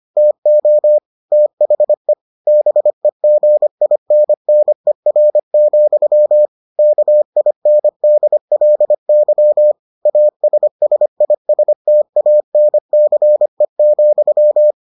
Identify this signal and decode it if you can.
Morse code, heard as a keyed tone, about 25 wpm, 600 Hz. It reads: TO THE BEGINNER, KINDLY ASSISTANCE,